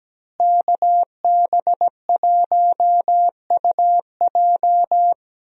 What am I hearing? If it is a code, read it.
Morse code, heard as a keyed tone, about 17 wpm, 700 Hz. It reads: KB1UJ
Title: USA Call Signs - 5 Characters - Repeated 3x - 17wpm